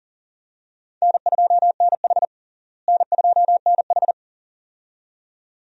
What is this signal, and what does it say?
Morse code, heard as a keyed tone, about 37 wpm, 700 Hz. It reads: N2NH N2NH
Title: USA Call Signs - 4 Characters - Repeated 2x - 40wpm